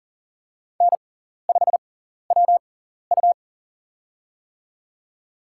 Morse code, heard as a keyed tone, about 40 wpm, spaced as for 14 wpm, 700 Hz. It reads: N5WU
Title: USA Call Signs - 4 Characters - 6x Character Spacing - 40wpm